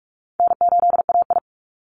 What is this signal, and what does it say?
Morse code, heard as a keyed tone, about 44 wpm, 700 Hz. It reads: N8AI
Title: Call Signs Repeated 2x - 45wpm